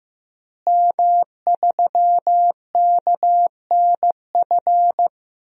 Morse code, heard as a keyed tone, about 16 wpm, 700 Hz. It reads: M3KNF
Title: DXCC-100 Call Signs - 5 Characters - Repeated 2x - 15wpm